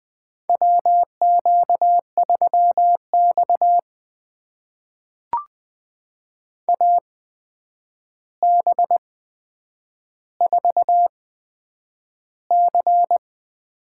Morse code, heard as a keyed tone, about 20 wpm, spaced as for 9 wpm, 700 Hz. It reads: WQ3X AB4C
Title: USA Call Signs - 8x Character Spacing - 20wpm